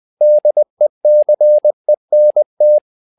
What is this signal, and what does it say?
Morse code, heard as a keyed tone, about 20 wpm, 600 Hz. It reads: DECENT
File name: tower trouble 20wpm